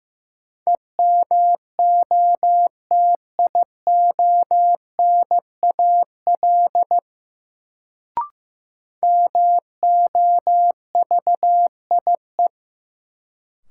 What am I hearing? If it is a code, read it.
Morse code, heard as a keyed tone, about 15 wpm, 700 Hz. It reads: EMOTIONAL MOVIE
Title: Sentences from Top 1000 Words - Review Part 5 - New Word Drill - 15wpm